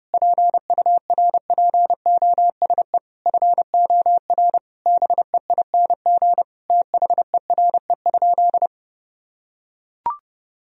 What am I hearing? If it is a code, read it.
Morse code, heard as a keyed tone, about 30 wpm, 700 Hz. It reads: PURPOSE FOR BEING THERE?
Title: Sentences from Top 1000 Words - Review Part 2 - 30wpm